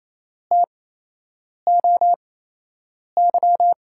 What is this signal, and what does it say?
Morse code, heard as a keyed tone, about 28 wpm, spaced as for 8 wpm, 700 Hz. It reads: TOY
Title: USA Call Signs - 5 Characters - 8x Character Spacing - 28wpm